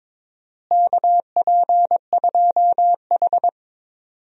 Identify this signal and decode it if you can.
Morse code, heard as a keyed tone, about 22 wpm, 700 Hz. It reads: KP2H
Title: DXCC-100 Call Signs - 4 Characters - Repeated 2x - 22wpm